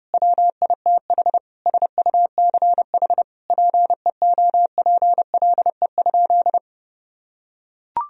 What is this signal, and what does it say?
Morse code, heard as a keyed tone, about 32 wpm, 700 Hz. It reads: WITH SUCH PEOPLE?
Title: Sentences from Top 1000 Words - Review Part 3 - 30wpm